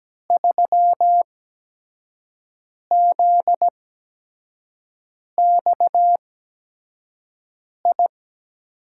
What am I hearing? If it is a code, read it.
Morse code, heard as a keyed tone, about 17 wpm, spaced as for 5 wpm, 700 Hz. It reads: 3ZXI
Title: USA Call Signs - 5 Characters - 8x Character Spacing - 17wpm